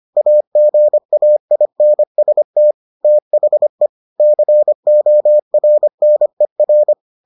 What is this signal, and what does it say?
Morse code, heard as a keyed tone, about 25 wpm, 600 Hz. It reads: AGAINST THE CORNER